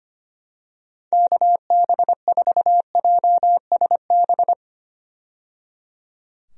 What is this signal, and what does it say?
Morse code, heard as a keyed tone, about 25 wpm, 700 Hz. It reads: KB4JSB
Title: USA Call Signs - 6 Characters - 25wpm